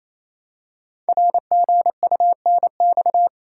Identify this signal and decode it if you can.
Morse code, heard as a keyed tone, about 27 wpm, 700 Hz. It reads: RGUNX
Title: Code Groups - 5 Letters - 28wpm